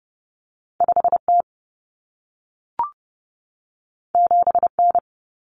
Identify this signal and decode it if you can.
Morse code, heard as a keyed tone, about 30 wpm, 700 Hz. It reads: 5T 7N
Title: Common Cut Numbers - 2-Digit 30wpm